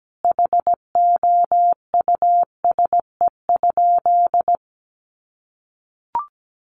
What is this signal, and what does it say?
Morse code, heard as a keyed tone, about 17 wpm, 700 Hz. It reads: HOUSE?
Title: Sentences from Top 1400 Words 17wpm